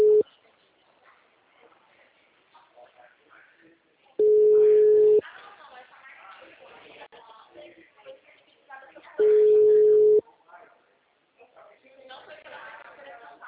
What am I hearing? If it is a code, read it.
Ringback tone (Europe, 425 Hz), with speech.